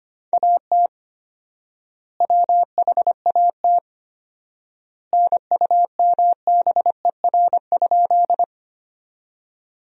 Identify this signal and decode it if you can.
Morse code, heard as a keyed tone, about 25 wpm, 700 Hz. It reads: AT WHAT NUMBER?
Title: 4x Word-Spacing - Sentences from Top 100 Words - 25wpm